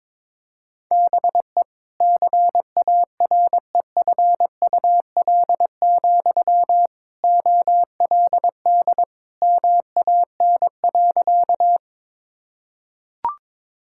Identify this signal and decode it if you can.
Morse code, heard as a keyed tone, about 22 wpm, 700 Hz. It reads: BE CAREFUL, OLD MAN.
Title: Sentences from Top 1600 Words 22wpm